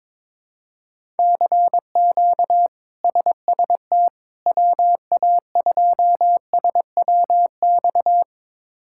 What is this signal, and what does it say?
Morse code, heard as a keyed tone, about 22 wpm, 700 Hz. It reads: CQ SST WA2SWX
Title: K1USN SST QSO Practice - 22wpm